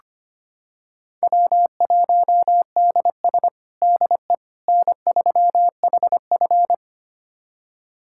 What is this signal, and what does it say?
Morse code, heard as a keyed tone, about 25 wpm, 700 Hz. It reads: W1DS DE N3HF
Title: USA Callsign DE Callsign - 4 Characters - 25wpm